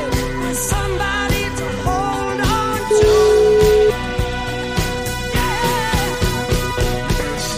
Ringback tone (Europe, 425 Hz), with noise.